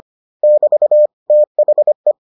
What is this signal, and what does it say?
Morse code, heard as a keyed tone, about 25 wpm, 600 Hz. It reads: <BT> THE